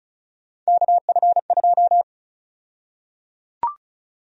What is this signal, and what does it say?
Morse code, heard as a keyed tone, about 36 wpm, 700 Hz. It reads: KF2
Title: USA Call Sign Prefixes - Repeated 2x - 35wpm